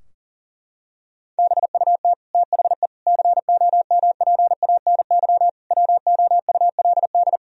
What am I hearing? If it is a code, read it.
Morse code, heard as a keyed tone, about 40 wpm, 700 Hz. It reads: BUT THE COMPANY WOULD
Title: Sentences from Top 1000 Words - Review Part 3 - 40wpm